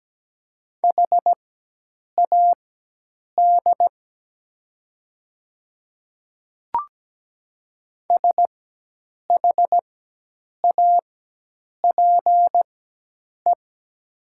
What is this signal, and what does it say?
Morse code, heard as a keyed tone, about 17 wpm, spaced as for 8 wpm, 700 Hz. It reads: HAD SHAPE
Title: 4x Character Spacing - Top 500 Words - 17wpm